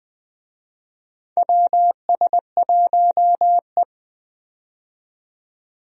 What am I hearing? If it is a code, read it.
Morse code, heard as a keyed tone, about 21 wpm, 700 Hz. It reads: WS1E
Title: USA Call Signs - 4 Characters - 20wpm